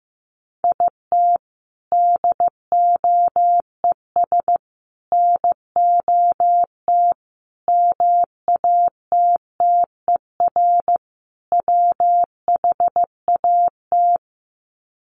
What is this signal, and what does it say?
Morse code, heard as a keyed tone, about 15 wpm, 700 Hz. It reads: IT DOES NOT MATTER WHAT
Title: Sets of 5 Words Encore 15wpm